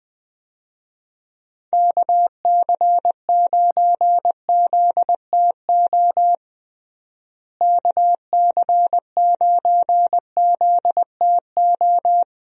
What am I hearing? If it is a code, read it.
Morse code, heard as a keyed tone, about 21 wpm, 700 Hz. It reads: KC9ZTO KC9ZTO
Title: USA Call Signs - 6 Characters - Repeated 3x - 20wpm